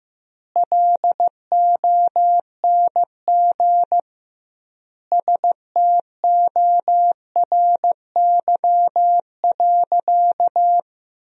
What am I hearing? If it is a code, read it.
Morse code, heard as a keyed tone, about 15 wpm, 700 Hz. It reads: LONG STORY.
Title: 2x Word-Spacing - Sentences from Top 300 Words - 15wpm